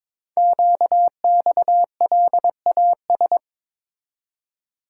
Morse code, heard as a keyed tone, about 22 wpm, 700 Hz. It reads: QXLAS
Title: Code Groups - 5 Letters - 22wpm